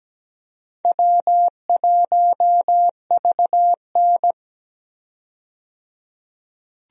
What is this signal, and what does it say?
Morse code, heard as a keyed tone, about 17 wpm, 700 Hz. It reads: W1VN
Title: USA Call Signs - 4 Characters - 17wpm